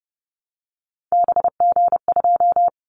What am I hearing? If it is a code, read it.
Morse code, heard as a keyed tone, about 31 wpm, 700 Hz. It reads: BG2
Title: Callsign Prefixes for Top 100 DXCC - 30wpm